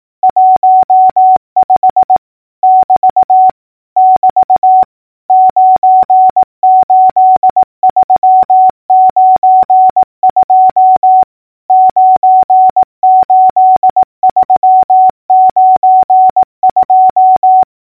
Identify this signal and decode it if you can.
Morse code, heard as a keyed tone, about 18 wpm, 750 Hz. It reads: 15 <BT> <BT> 98392 98392
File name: numbers / V2-morse-clicks